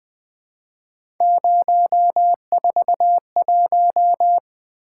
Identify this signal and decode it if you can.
Morse code, heard as a keyed tone, about 21 wpm, 700 Hz. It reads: 041
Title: Three Letter-Number Combinations - 20wpm